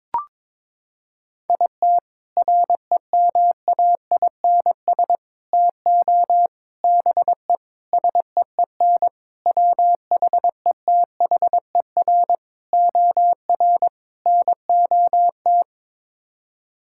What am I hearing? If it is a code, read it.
Morse code, heard as a keyed tone, about 21 wpm, 700 Hz. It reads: IT REMAINS TO BE SEEN WHETHER OR NOT
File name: practice / 22wpm / Sets of 8 Words - 22wpm